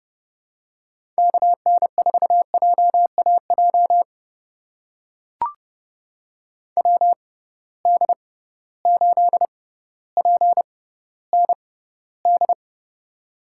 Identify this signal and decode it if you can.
Morse code, heard as a keyed tone, about 32 wpm, spaced as for 16 wpm, 700 Hz. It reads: KN4JAJ WD8PND
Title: USA Call Signs - 6 Characters - 6x Character Spacing - 30wpm